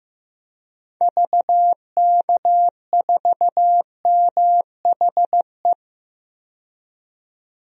Morse code, heard as a keyed tone, about 15 wpm, 700 Hz. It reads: VK4MHE
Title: DXCC-100 Call Signs - 6 Characters - 15wpm